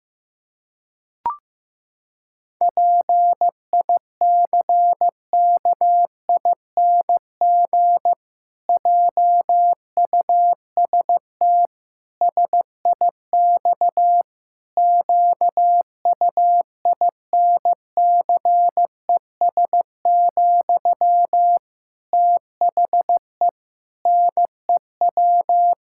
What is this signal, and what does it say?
Morse code, heard as a keyed tone, about 15 wpm, 700 Hz. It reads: PICKING JUST SIX QUINCES, THE NEW
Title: Pangram Challenge v2 - 15wpm